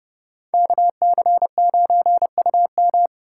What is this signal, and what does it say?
Morse code, heard as a keyed tone, about 29 wpm, 700 Hz. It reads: KC9UM